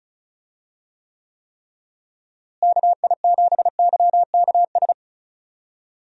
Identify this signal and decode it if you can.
Morse code, heard as a keyed tone, about 36 wpm, 700 Hz. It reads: KI7YKS
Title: USA Call Signs - 6 Characters - Repeated 3x - 35wpm